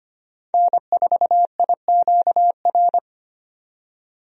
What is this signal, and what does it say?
Morse code, heard as a keyed tone, about 25 wpm, 700 Hz. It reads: N4IQR
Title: USA Call Signs - 5 Characters - 25wpm